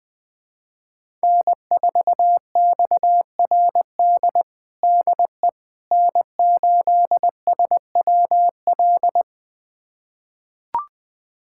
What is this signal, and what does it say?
Morse code, heard as a keyed tone, about 21 wpm, 700 Hz. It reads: N4XRD DE N8SWL
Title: USA Callsign DE Callsign 20wpm